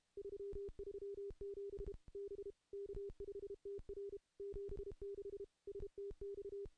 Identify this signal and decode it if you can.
Morse code, heard as a keyed tone, about 32 wpm, 400 Hz. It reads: 337 B K5TR 76 STX
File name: contests / CW-SS-k5tr- -0000